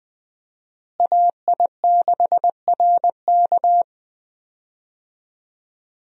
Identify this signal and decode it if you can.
Morse code, heard as a keyed tone, about 21 wpm, 700 Hz. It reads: AI6RK
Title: USA Call Signs - 5 Characters - Repeated 3x - 20wpm